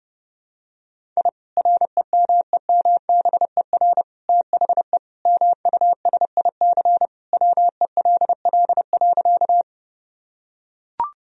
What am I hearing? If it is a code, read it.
Morse code, heard as a keyed tone, about 30 wpm, 700 Hz. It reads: I REMEMBER THE MUSIC WELL.
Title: Sentences from Top 1000 Words - Review Part 1 - 30wpm